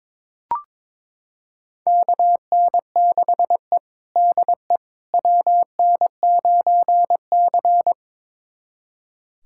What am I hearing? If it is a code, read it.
Morse code, heard as a keyed tone, about 21 wpm, 700 Hz. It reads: KN6E DE WN9C
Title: USA Callsign DE Callsign - 4 Characters - 22wpm